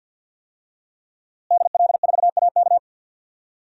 Morse code, heard as a keyed tone, about 51 wpm, 700 Hz. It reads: DL4AK
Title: DXCC-100 Call Signs - 5 Characters - Repeated 2x - 50wpm